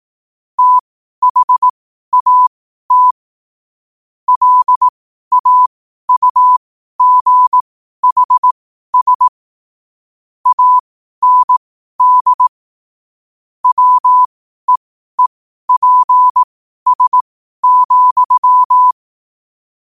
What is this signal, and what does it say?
Morse code, heard as a keyed tone, about 18 wpm, spaced as for 12 wpm, 1000 Hz. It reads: THAT LAUGHS AND WEEPS,
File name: qotdfiles / 12wpm